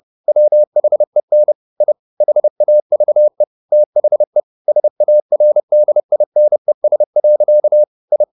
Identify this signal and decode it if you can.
Morse code, heard as a keyed tone, about 30 wpm, 600 Hz. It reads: WHEN I HAVE THE SARDINES. I
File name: the old man and the sea / the old man and the sea 1 30wpm